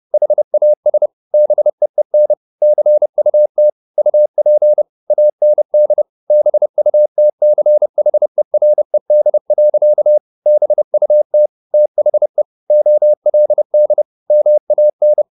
Morse code, heard as a keyed tone, about 30 wpm, 600 Hz. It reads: HAS BEEN CUT UP AND BUTCHERED. BUT THE OLD MAN
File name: the old man and the sea / the old man and the sea 1 30wpm